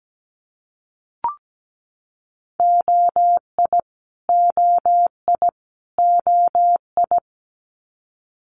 Morse code, heard as a keyed tone, about 17 wpm, 700 Hz. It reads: OI OI OI
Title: Multi-letter Phonemes Repeated 3x 17wpm